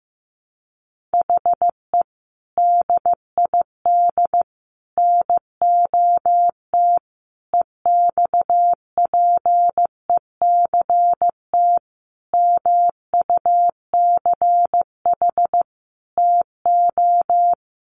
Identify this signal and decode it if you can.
Morse code, heard as a keyed tone, about 15 wpm, 700 Hz. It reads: HE DID NOT EXPECT MUCH TO
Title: Sentences from Top 1100 Words 15wpm